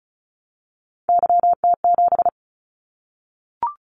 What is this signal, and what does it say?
Morse code, heard as a keyed tone, about 36 wpm, 700 Hz. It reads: YT7
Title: Callsign Prefixes for Top 25 DXCC - 35wpm